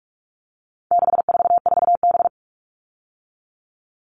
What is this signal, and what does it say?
Morse code, heard as a keyed tone, about 44 wpm, 700 Hz. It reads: 644B
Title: All Cut Numbers - 4-Digit 45wpm